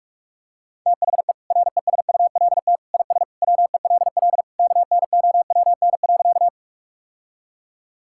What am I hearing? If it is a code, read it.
Morse code, heard as a keyed tone, about 42 wpm, 700 Hz. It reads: THE RESULT IS WELL KNOWN.
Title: Sentences from Top 1000 Words - Review Part 5 - 45wpm